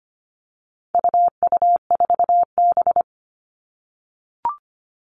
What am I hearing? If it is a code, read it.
Morse code, heard as a keyed tone, about 25 wpm, 700 Hz. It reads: UU4B
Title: All Cut Numbers - 4-Digit 25wpm